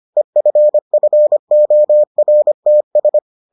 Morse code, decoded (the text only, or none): EFFORTS